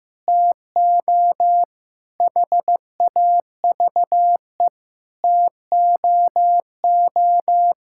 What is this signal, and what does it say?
Morse code, heard as a keyed tone, about 15 wpm, 700 Hz. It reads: TO HAVE TOO
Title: Sentences from Top 1000 Words - Review Part 5 - 15wpm